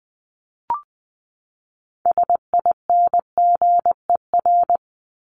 Morse code, heard as a keyed tone, about 20 wpm, 700 Hz. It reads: SINGER
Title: Top 1501-1750 Words 20wpm